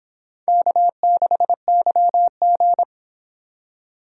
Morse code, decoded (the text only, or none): K6YG